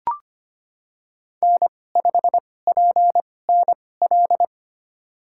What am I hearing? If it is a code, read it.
Morse code, heard as a keyed tone, about 25 wpm, spaced as for 18 wpm, 700 Hz. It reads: N5PNL
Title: USA Call Signs - 5 Characters - 2x Character Spacing - 25wpm